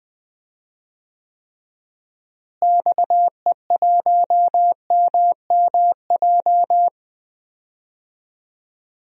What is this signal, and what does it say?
Morse code, heard as a keyed tone, about 20 wpm, 700 Hz. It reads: XE1MMJ
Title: DXCC-100 Call Signs - 6 Characters - Repeated 2x - 20wpm